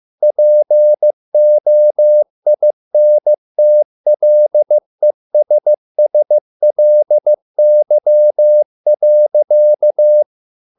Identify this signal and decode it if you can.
Morse code, heard as a keyed tone, about 15 wpm, 600 Hz. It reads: POINTLESSLY.